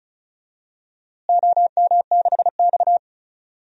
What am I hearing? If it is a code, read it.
Morse code, heard as a keyed tone, about 36 wpm, 700 Hz. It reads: OM6X
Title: DXCC-100 Call Signs - 4 Characters - Repeated 3x - 35wpm